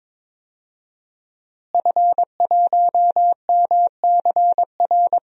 Morse code, heard as a keyed tone, about 22 wpm, 700 Hz. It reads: F1MCR
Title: DXCC-100 Call Signs - 5 Characters - Repeated 2x - 22wpm